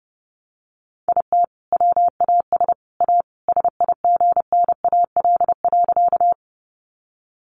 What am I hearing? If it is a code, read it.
Morse code, heard as a keyed tone, about 30 wpm, 700 Hz. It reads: IT WAS A SIGNAL.
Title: Sentences from Top 1400 Words 30wpm